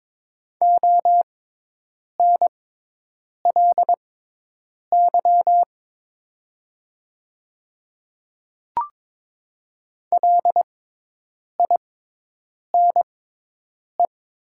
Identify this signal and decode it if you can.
Morse code, heard as a keyed tone, about 22 wpm, spaced as for 8 wpm, 700 Hz. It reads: ONLY LINE
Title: 6x Character Spacing - Top 300 Words - 22wpm